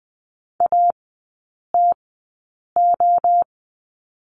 Morse code, heard as a keyed tone, about 20 wpm, spaced as for 8 wpm, 700 Hz. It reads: ATO
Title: 074 Lesson 13e - Letter L - Three Characters 20wpm